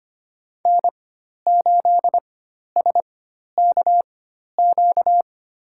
Morse code, heard as a keyed tone, about 25 wpm, spaced as for 12 wpm, 700 Hz. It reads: N8SKQ
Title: USA Call Signs - 5 Characters - 4x Character Spacing - 25wpm